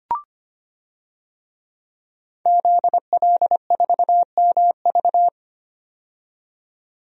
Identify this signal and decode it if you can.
Morse code, heard as a keyed tone, about 25 wpm, 700 Hz. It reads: ZL4MV
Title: DXCC-100 Call Signs - 5 Characters - 25wpm